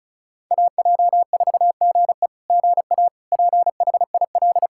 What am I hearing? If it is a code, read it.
Morse code, heard as a keyed tone, about 34 wpm, 690 Hz. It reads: AJ4GE GA PHIL
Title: SOTA QSO Practice 35wpm